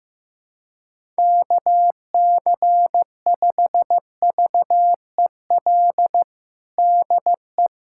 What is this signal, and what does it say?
Morse code, heard as a keyed tone, about 15 wpm, 700 Hz. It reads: KC5VEL DE
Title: USA Callsign DE Callsign - 6 Characters - 15wpm